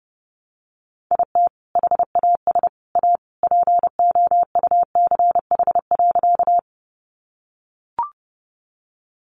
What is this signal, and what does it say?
Morse code, heard as a keyed tone, about 28 wpm, 700 Hz. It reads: IT HAS A POUCH.
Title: Easy Phrases 30wpm